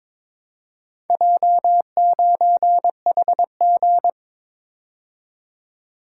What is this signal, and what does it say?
Morse code, heard as a keyed tone, about 22 wpm, 700 Hz. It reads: J9HG
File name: practice / 22wpm / Four Letter-Number Combinations - 22wpm